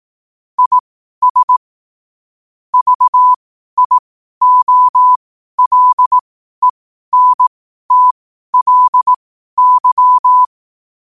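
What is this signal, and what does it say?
Morse code, heard as a keyed tone, about 18 wpm, spaced as for 12 wpm, 1000 Hz. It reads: IS VIOLENTLY
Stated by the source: Morse code quote of the day at 12 wpm.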